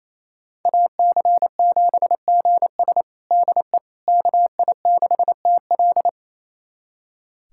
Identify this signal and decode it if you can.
Morse code, heard as a keyed tone, about 28 wpm, 700 Hz. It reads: AC7GS DE KI6TL
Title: USA Callsign DE Callsign - 5 Characters - 28wpm